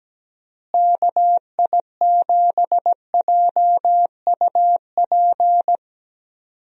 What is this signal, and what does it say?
Morse code, heard as a keyed tone, about 17 wpm, 700 Hz. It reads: KI7JUP
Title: USA Call Signs - 6 Characters - Repeated 2x - 17wpm